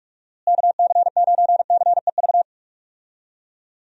Morse code, heard as a keyed tone, about 46 wpm, 700 Hz. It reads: KC9CEV
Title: USA Call Signs - 6 Characters - Repeated 2x - 45wpm